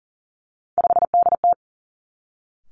Morse code, heard as a keyed tone, about 42 wpm, 700 Hz. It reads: 5DT
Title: Three Challenging Letter-Number Combinations 40wpm